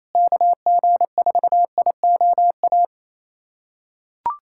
Morse code, heard as a keyed tone, about 28 wpm, 700 Hz. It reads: KG4IOA
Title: USA Callsign DE Callsign 28wpm